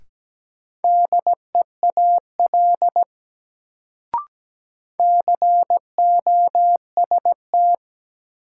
Morse code, heard as a keyed tone, about 17 wpm, 700 Hz. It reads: DEAL COST
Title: Sets of 3 Words - New Word Drill - 17wpm